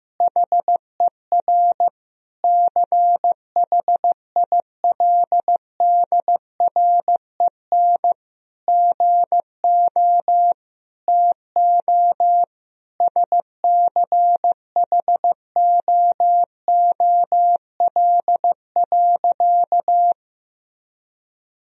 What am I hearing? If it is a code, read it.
Morse code, heard as a keyed tone, about 15 wpm, 700 Hz. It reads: HER CHILDREN GO TO SCHOOL.
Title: Sentences from Top 1000 Words - Review Part 4 - 15wpm